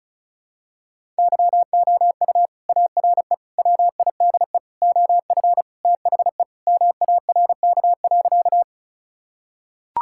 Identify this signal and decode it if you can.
Morse code, heard as a keyed tone, about 34 wpm, 700 Hz. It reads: YOU ARE WIDE OF THE MARK.
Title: Sentences from Top 1000 Words - Review Part 6 - 35wpm